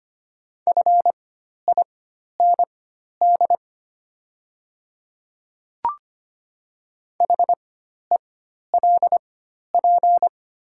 Morse code, heard as a keyed tone, about 25 wpm, spaced as for 12 wpm, 700 Hz. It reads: FIND HELP
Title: 4x Character Spacing - Top 200 Words - 25wpm